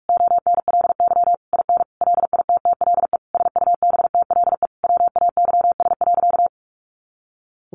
Morse code, heard as a keyed tone, about 44 wpm, 700 Hz. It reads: ONLY IN LITTLE SUBTLE WAYS.
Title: A Sound Of Thunder 45wpm